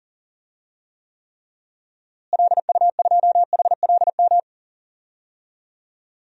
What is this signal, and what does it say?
Morse code, heard as a keyed tone, about 40 wpm, 700 Hz. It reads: LU2HLM
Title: DXCC-100 Call Signs - 6 Characters - Repeated 3x - 40wpm